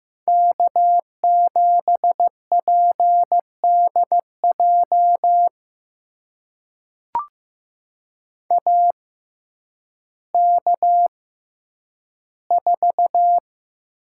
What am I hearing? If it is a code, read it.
Morse code, heard as a keyed tone, about 15 wpm, spaced as for 9 wpm, 700 Hz. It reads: K7PDJ AK4
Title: USA Call Signs - 5 Characters - 6x Character Spacing - 15wpm